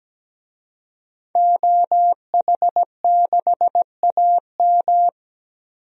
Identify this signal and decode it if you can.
Morse code, heard as a keyed tone, about 17 wpm, 700 Hz. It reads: OH6AM
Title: DXCC-100 Call Signs - 5 Characters - 17wpm